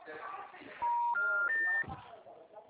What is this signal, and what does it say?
Special information tone, with speech.